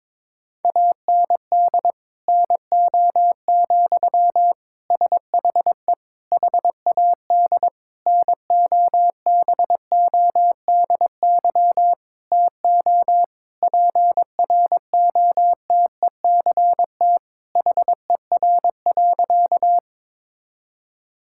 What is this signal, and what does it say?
Morse code, heard as a keyed tone, about 22 wpm, 700 Hz. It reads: AND NO, SHE HAD NOBODY TO PROTECT HER.
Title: Sentences from Top 1600 Words 22wpm